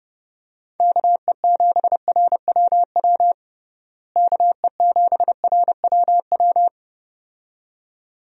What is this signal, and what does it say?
Morse code, heard as a keyed tone, about 32 wpm, 700 Hz. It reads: KE7RWW KE7RWW
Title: USA Call Signs - 6 Characters - Repeated 2x - 30wpm